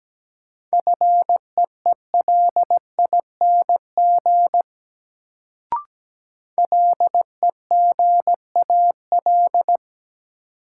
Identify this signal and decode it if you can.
Morse code, heard as a keyed tone, about 17 wpm, 700 Hz. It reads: FEELING LEGAL
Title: Sentences from Top 1200 Words - New Word Drill Alt - 17wpm